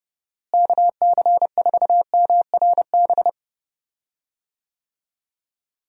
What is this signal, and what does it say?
Morse code, heard as a keyed tone, about 30 wpm, 700 Hz. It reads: KC4MRB
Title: USA Call Signs - 6 Characters - Repeated 2x - 30wpm